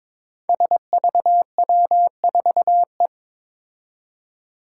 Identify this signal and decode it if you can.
Morse code, heard as a keyed tone, about 22 wpm, 700 Hz. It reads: SVW4E
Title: Five Challenging Letter-Number Combinations 22wpm